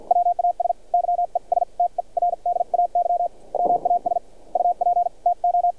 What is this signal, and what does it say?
Morse code, heard as a keyed tone, about 51 wpm, spaced as for 42 wpm, 700 Hz, with radio noise. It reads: WAS YESTERDAY MAS UP TO
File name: archived / CRSnet